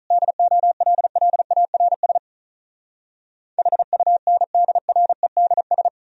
Morse code, heard as a keyed tone, about 46 wpm, 700 Hz. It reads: DOLLARS HUNDREDS